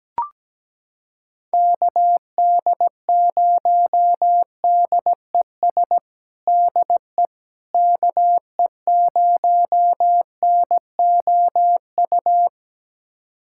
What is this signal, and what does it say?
Morse code, heard as a keyed tone, about 17 wpm, 700 Hz. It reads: KD0DES DE KE0NOU
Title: USA Callsign DE Callsign - 6 Characters - 17wpm